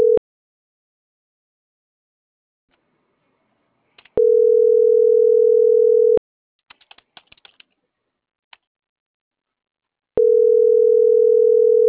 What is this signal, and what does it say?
Ringback tone (North America), 440 + 480 Hz.